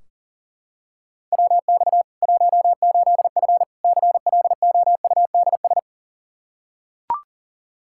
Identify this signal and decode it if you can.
Morse code, heard as a keyed tone, about 40 wpm, 700 Hz. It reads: WX 18F CLOUDS
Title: Practice QSO - Exchange 2 - Info - 40wpm